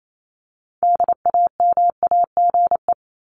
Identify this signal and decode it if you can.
Morse code, heard as a keyed tone, about 28 wpm, 700 Hz. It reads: DAMAGE
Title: Top 1001-1250 Words 28wpm